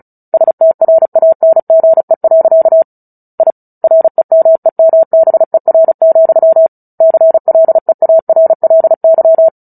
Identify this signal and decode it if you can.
Morse code, heard as a keyed tone, about 34 wpm, 650 Hz. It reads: STRANGE. I REMEMBER, CLEARLY
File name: Book Two / 09-Chapter 9 - 35 WPM